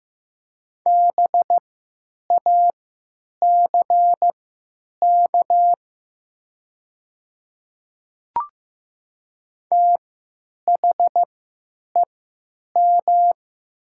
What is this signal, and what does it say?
Morse code, heard as a keyed tone, about 15 wpm, spaced as for 9 wpm, 700 Hz. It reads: BACK THEM
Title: 3x Character Spacing - Top 200 Words - 15wpm